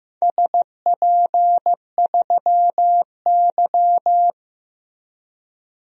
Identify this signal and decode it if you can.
Morse code, heard as a keyed tone, about 15 wpm, 700 Hz. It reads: SP3Y
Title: DXCC-100 Call Signs - 4 Characters - 15wpm